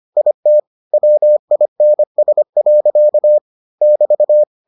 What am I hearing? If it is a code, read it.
Morse code, heard as a keyed tone, about 25 wpm, 600 Hz. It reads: IT WINS. <BT>